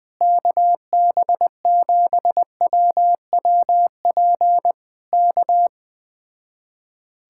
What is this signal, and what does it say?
Morse code, heard as a keyed tone, about 20 wpm, 700 Hz. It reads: KB7WWP K
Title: Speed-Racing Common QSO Phrases - 17wpm